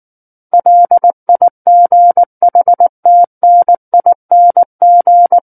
Morse code, heard as a keyed tone, about 19 wpm, 700 Hz. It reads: LIGHTNING
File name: Book One / 10-Chapter 10 - 19 WPM